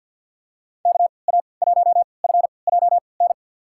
Morse code, heard as a keyed tone, about 52 wpm, spaced as for 29 wpm, 700 Hz. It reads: KA1FJN
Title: USA Call Signs - 6 Characters - 3x Character Spacing - 50wpm